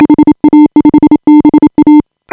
Morse code, heard as a keyed tone, about 27 wpm, 310 Hz. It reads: HA5BA